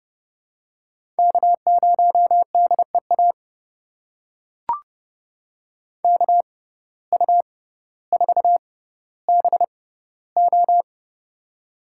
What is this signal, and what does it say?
Morse code, heard as a keyed tone, about 32 wpm, spaced as for 16 wpm, 700 Hz. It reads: K0DEA KU4BO
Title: USA Call Signs - 5 Characters - 6x Character Spacing - 30wpm